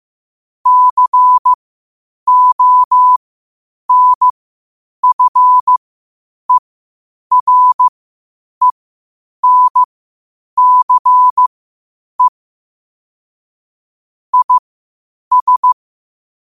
Morse code, heard as a keyed tone, about 15 wpm, spaced as for 8 wpm, 1000 Hz. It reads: CONFERENCE IS
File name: qotdfiles / 8wpm